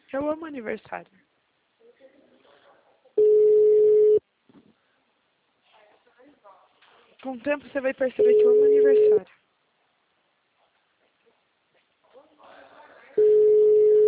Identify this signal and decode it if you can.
Ringback tone (Europe, 425 Hz), with speech.